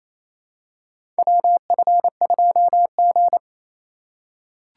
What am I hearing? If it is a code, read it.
Morse code, heard as a keyed tone, about 28 wpm, 700 Hz. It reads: WF2G
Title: USA Call Signs - 4 Characters - 2x Character Spacing - 28wpm